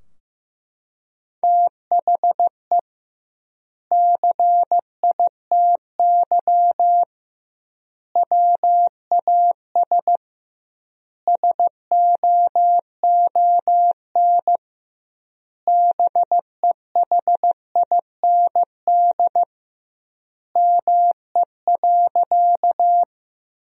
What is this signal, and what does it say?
Morse code, heard as a keyed tone, about 16 wpm, 700 Hz. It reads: THE CITY WAS SOON BEHIND ME.
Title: 2x Word-Spacing - Sentences from Top 500 Words - 15wpm